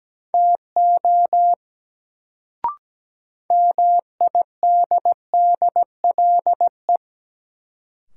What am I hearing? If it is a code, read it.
Morse code, heard as a keyed tone, about 17 wpm, 700 Hz. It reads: TO MIDDLE
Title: Sets of 2 Words - New Word Drill - 17wpm